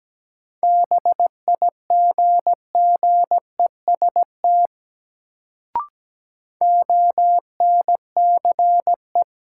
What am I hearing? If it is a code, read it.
Morse code, heard as a keyed tone, about 17 wpm, 700 Hz. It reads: BIGGEST ONCE
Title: Sets of 4 Words - New Word Drill - 17wpm